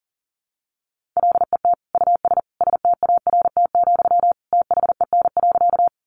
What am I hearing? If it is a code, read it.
Morse code, heard as a keyed tone, about 37 wpm, 700 Hz. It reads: LET US START, THEN.
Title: Sentences from Top 300 Words 40wpm